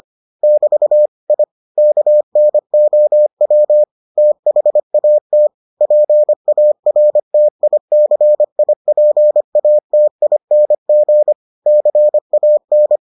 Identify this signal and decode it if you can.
Morse code, heard as a keyed tone, about 25 wpm, 600 Hz. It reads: <BT> I KNOW THAT PARTICIPATING CAN